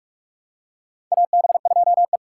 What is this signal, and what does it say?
Morse code, heard as a keyed tone, about 44 wpm, 700 Hz. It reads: AB2E
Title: USA Call Signs - 4 Characters - 3x Character Spacing - 45wpm